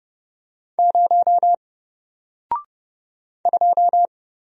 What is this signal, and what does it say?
Morse code, heard as a keyed tone, about 30 wpm, 700 Hz. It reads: 0 2
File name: Single-Letter-Number-30wpm